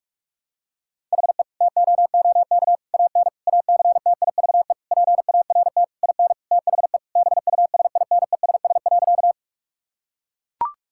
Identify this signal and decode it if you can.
Morse code, heard as a keyed tone, about 43 wpm, 700 Hz. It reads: HE TOOK AN ACTIVE PART IN THE BUSINESS.
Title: Sentences from Top 1000 Words - Review Part 6 - 45wpm